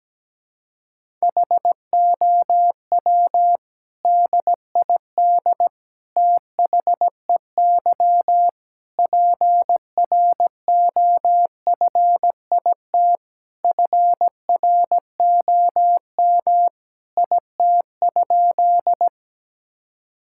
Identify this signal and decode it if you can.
Morse code, heard as a keyed tone, about 17 wpm, 700 Hz. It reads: HOW DID THEY PROFIT FROM IT?
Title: Sentences from Top 1000 Words - Review Part 5 - 17wpm